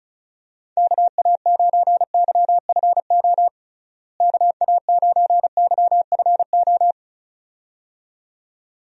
Morse code, heard as a keyed tone, about 36 wpm, 700 Hz. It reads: KA9YFO KA9YFO
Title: USA Call Signs - 6 Characters - Repeated 2x - 35wpm